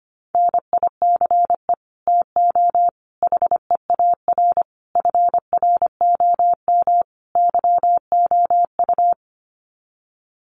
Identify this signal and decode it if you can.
Morse code, heard as a keyed tone, about 25 wpm, 700 Hz. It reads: NICE TO HEAR FROM YOU
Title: Common QSO Phrases 25wpm